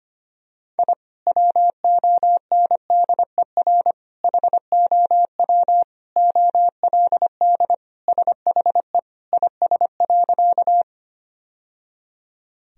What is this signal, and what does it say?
Morse code, heard as a keyed tone, about 25 wpm, 700 Hz. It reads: I WONDER HOW OLD SHE IS.
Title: Sentences from Top 1000 Words - Review Part 3 - 25wpm